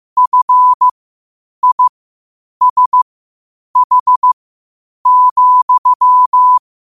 Morse code, heard as a keyed tone, about 15 wpm, spaced as for 8 wpm, 1000 Hz. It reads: FISH,
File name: qotdfiles / 8wpm